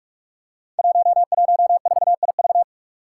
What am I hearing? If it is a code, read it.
Morse code, heard as a keyed tone, about 44 wpm, 700 Hz. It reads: 114IV
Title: Five Challenging Letter-Number Combinations 45wpm